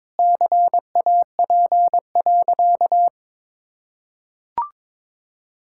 Morse code, heard as a keyed tone, about 22 wpm, 700 Hz. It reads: CAP.
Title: Sentences from Top 1600 Words 22wpm